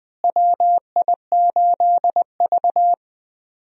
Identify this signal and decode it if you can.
Morse code, heard as a keyed tone, about 19 wpm, 700 Hz. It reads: WI8V
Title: USA Call Signs - 4 Characters - Repeated 3x - 20wpm